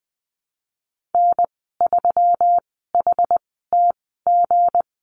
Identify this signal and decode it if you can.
Morse code, heard as a keyed tone, about 20 wpm, spaced as for 15 wpm, 700 Hz. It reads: N3HTG
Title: 167 Lesson 27g - Number 7 - Call Signs 20wpm